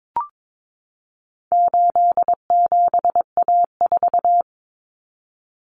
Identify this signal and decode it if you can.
Morse code, heard as a keyed tone, about 22 wpm, 700 Hz. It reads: 87A4
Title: Common Cut Numbers - 4-Digit 22wpm